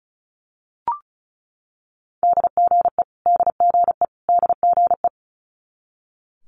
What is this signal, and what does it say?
Morse code, heard as a keyed tone, about 36 wpm, 700 Hz. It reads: DGE DGE DGE
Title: Multi-letter Phonemes Repeated 3x 35wpm